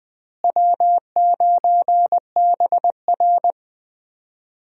Morse code, heard as a keyed tone, about 20 wpm, 700 Hz. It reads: W9BR
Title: USA Call Signs - 4 Characters - Repeated 3x - 20wpm